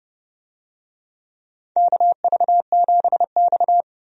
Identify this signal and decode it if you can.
Morse code, heard as a keyed tone, about 32 wpm, 700 Hz. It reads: KV7X
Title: USA Call Signs - 4 Characters - 30wpm